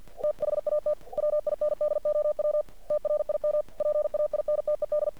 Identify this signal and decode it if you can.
Morse code, heard as a keyed tone, about 49 wpm, 600 Hz. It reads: THAT WINDOW TRIM PAINTED
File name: archived / CRSnet